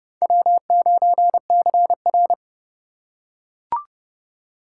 Morse code, heard as a keyed tone, about 27 wpm, 700 Hz. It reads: W9CR